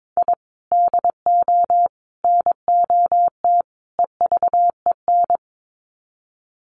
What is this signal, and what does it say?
Morse code, heard as a keyed tone, about 22 wpm, 700 Hz. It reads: I DO NOT EVEN
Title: Sets of 4 Words 22wpm